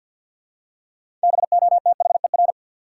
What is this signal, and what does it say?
Morse code, heard as a keyed tone, about 52 wpm, 700 Hz. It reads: BOTHER
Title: Six Letter Words - 50wpm